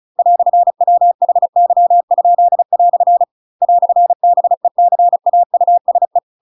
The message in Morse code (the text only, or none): "WHY?" "BECAUSE